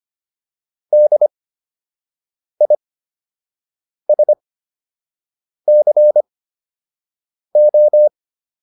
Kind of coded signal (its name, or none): Morse code